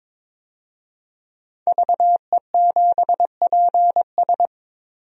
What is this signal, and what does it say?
Morse code, heard as a keyed tone, about 21 wpm, 700 Hz. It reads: VE7PS
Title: DXCC-100 Call Signs - 5 Characters - 22wpm